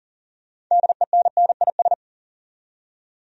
Morse code, heard as a keyed tone, about 40 wpm, 700 Hz. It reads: DENNIS
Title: USA Common Names - 40wpm